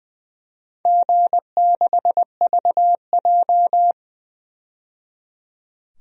Morse code, heard as a keyed tone, about 20 wpm, 700 Hz. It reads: G6VJ
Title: DXCC-100 Call Signs - 4 Characters - 20wpm